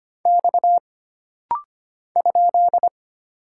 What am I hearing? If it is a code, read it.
Morse code, heard as a keyed tone, about 25 wpm, 700 Hz. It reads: X ?
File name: Single Character (Letter:Number:Punctuation:Prosign) - Rapid-Fire - 25wpm